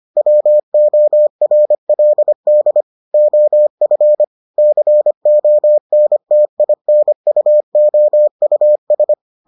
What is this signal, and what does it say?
Morse code, heard as a keyed tone, about 25 wpm, 600 Hz. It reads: WORLD OF CONTINUOUS